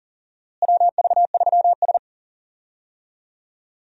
Morse code, heard as a keyed tone, about 40 wpm, 700 Hz. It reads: WV3S
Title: USA Call Signs - 4 Characters - Repeated 3x - 40wpm